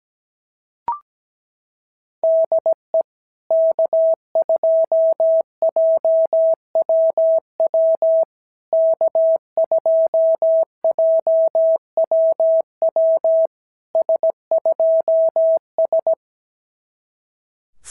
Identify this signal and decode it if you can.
Morse code, heard as a keyed tone, about 18 wpm, 650 Hz. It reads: DE K2JWW K2JWW S2S
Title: SOTA QSO Practice 17wpm